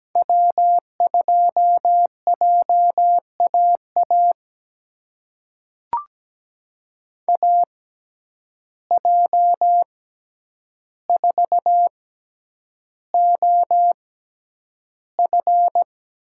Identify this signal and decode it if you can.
Morse code, heard as a keyed tone, about 17 wpm, spaced as for 9 wpm, 700 Hz. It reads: W2JAA AJ4OF
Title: USA Call Signs - 5 Characters - 6x Character Spacing - 17wpm